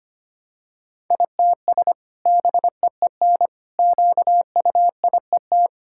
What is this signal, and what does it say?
Morse code, heard as a keyed tone, about 25 wpm, 700 Hz. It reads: ITS BEEN QUIET
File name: Sentences with 2-Word Phrases - 25wpm